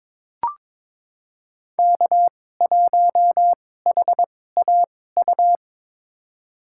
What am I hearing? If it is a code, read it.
Morse code, heard as a keyed tone, about 22 wpm, spaced as for 16 wpm, 700 Hz. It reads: K1HAU
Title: USA Call Signs - 5 Characters - 2x Character Spacing - 22wpm